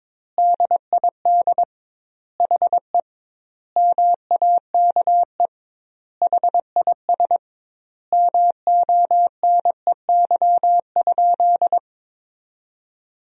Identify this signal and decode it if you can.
Morse code, heard as a keyed tone, about 22 wpm, 700 Hz. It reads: DID HE MAKE HIS MONEY?
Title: 2x Word-Spacing - Sentences from Top 400 Words - 22wpm